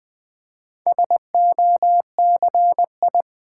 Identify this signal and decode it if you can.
Morse code, heard as a keyed tone, about 21 wpm, 700 Hz. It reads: SOCI
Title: Build A Word - 2 Letters At A Time - 6-Letter Words - 20wpm